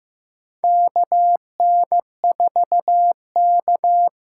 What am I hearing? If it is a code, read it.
Morse code, heard as a keyed tone, about 15 wpm, 700 Hz. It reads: KN4K
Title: USA Call Signs - 6 Characters - 6x Character Spacing - 15wpm